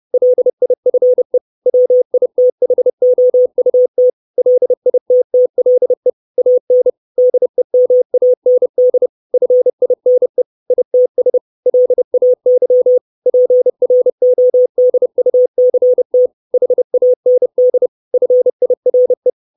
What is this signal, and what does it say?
Morse code, heard as a keyed tone, about 30 wpm, 500 Hz. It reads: LIFE WITHOUT LITTLE AN DEMAND FINE ITS LAY PRODUCT HAND FIRE
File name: 632 Words 30WPM